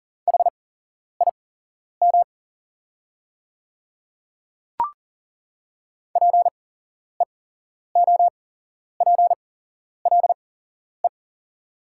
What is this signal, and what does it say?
Morse code, heard as a keyed tone, about 40 wpm, spaced as for 11 wpm, 700 Hz. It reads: HIM PEOPLE
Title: 8x Character Spacing - Top 100 Words - 40wpm